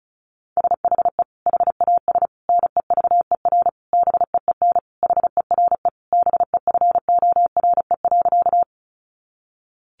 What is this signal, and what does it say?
Morse code, heard as a keyed tone, about 34 wpm, 700 Hz. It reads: SHE HAS NEVER BEEN HERE BEFORE.
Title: Additional Easy Phrases 35wpm